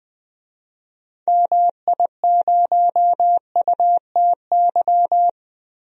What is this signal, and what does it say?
Morse code, heard as a keyed tone, about 20 wpm, 700 Hz. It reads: MI0UTY
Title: DXCC-100 Call Signs - 6 Characters - 20wpm